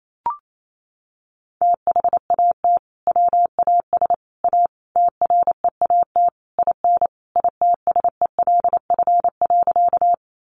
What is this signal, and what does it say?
Morse code, heard as a keyed tone, about 28 wpm, 700 Hz. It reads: THAT WAS A TREAT IN ITSELF.
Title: Sentences from Top 1300 Words 28wpm